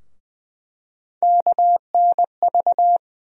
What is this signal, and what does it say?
Morse code, heard as a keyed tone, about 21 wpm, 700 Hz. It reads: KNV
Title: USA Call Sign Postfixes - 3 Letters 20wpm